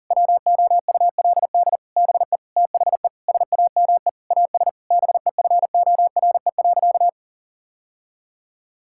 Morse code, heard as a keyed tone, about 40 wpm, 700 Hz. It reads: WOULD BE THE SAME AS BEFORE.
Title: Sentences from Top 1000 Words - Review Part 5 - 40wpm